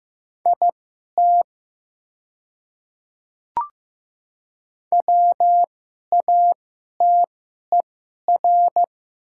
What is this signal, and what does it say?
Morse code, heard as a keyed tone, about 15 wpm, spaced as for 11 wpm, 700 Hz. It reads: IT WATER
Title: 2x Character Spacing - Top 100 Words - 15wpm